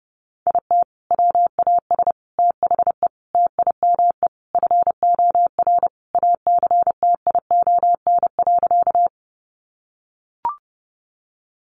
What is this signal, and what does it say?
Morse code, heard as a keyed tone, about 30 wpm, 700 Hz. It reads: IT WAS THE TIME FOR ACTION.
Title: Sentences from Top 700 Words 30wpm